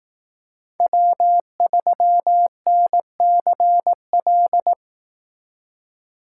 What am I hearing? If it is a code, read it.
Morse code, heard as a keyed tone, about 18 wpm, 700 Hz. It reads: W3NCL
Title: Speed Racing - USA 1x3 Call Signs 15wpm